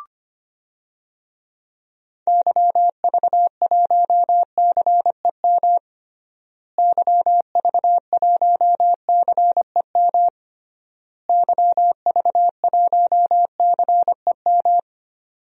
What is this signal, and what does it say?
Morse code, heard as a keyed tone, about 25 wpm, 700 Hz. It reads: YV1CEM YV1CEM YV1CEM
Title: DXCC-100 Call Signs - 6 Characters - Repeated 3x - 25wpm